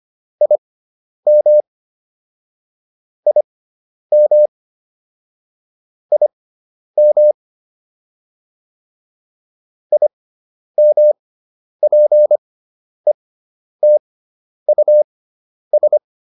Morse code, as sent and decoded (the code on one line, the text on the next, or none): .. -- / .. -- / .. -- / .. -- .--. - ..- ...
IM IM IM IMPETUS